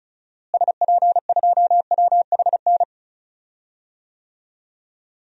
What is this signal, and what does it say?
Morse code, heard as a keyed tone, about 34 wpm, 700 Hz. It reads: SP2WHN
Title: DXCC-100 Call Signs - 6 Characters - Repeated 2x - 35wpm